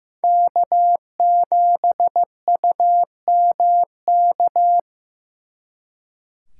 Morse code, heard as a keyed tone, about 18 wpm, 700 Hz. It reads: K7UMK
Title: Speed Racing - USA 1x3 Call Signs 15wpm